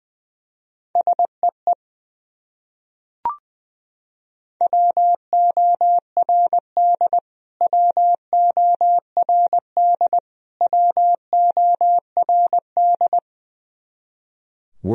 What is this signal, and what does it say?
Morse code, heard as a keyed tone, about 20 wpm, 700 Hz. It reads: SEE WORD WORD WORD
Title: Top 100 Words Repeated 20wpm